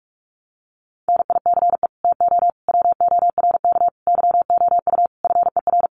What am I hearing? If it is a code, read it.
Morse code, heard as a keyed tone, about 44 wpm, 700 Hz. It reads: NICE TO WORK YOU FER
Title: Common QSO Phrases 45wpm